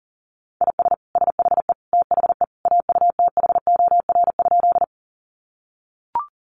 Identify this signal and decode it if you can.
Morse code, heard as a keyed tone, about 40 wpm, 700 Hz. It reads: IS SHE THE AUTHOR?
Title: Sentences from Top 1200 Words 40wpm